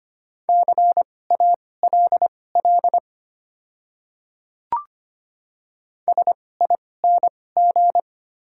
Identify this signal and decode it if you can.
Morse code, heard as a keyed tone, about 25 wpm, spaced as for 18 wpm, 700 Hz. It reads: CALL SING
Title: 2x Character Spacing - Top 500 Words - 25wpm